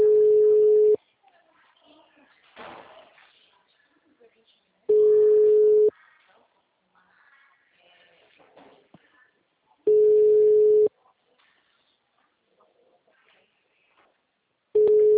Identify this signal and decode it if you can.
Ringback tone (Europe, 425 Hz), with speech.